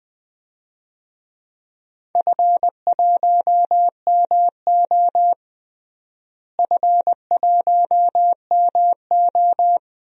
Morse code, heard as a keyed tone, about 20 wpm, 700 Hz. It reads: F1MO F1MO
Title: DXCC-100 Call Signs - 4 Characters - Repeated 3x - 20wpm